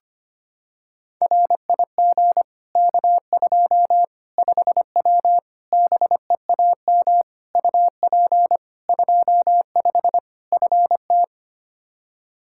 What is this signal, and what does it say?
Morse code, heard as a keyed tone, about 25 wpm, 700 Hz. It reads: RIG K2 5W BEAM UP 25 FT
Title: Practice QSO - Exchange 2 - Info - 25wpm